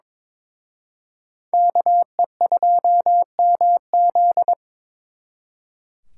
Morse code, heard as a keyed tone, about 25 wpm, 700 Hz. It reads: KE2MZ
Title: Speed Racing - USA 2x2 Call Signs 22wpm